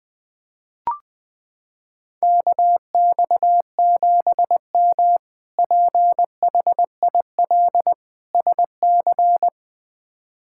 Morse code, heard as a keyed tone, about 20 wpm, 700 Hz. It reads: KX7M PHIL SC
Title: K1USN SST QSO Practice - 20wpm